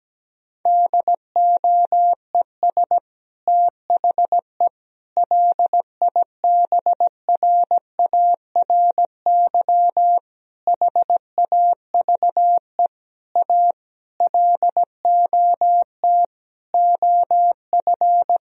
Morse code, heard as a keyed tone, about 17 wpm, 700 Hz. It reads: DOES THE LIBRARY HAVE A LOT OF
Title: Sentences from Top 1000 Words - Review Part 4 - 17wpm